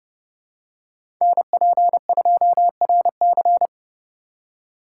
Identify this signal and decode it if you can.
Morse code, heard as a keyed tone, about 31 wpm, 700 Hz. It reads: NP2RC
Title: USA Call Signs - 5 Characters - 2x Character Spacing - 30wpm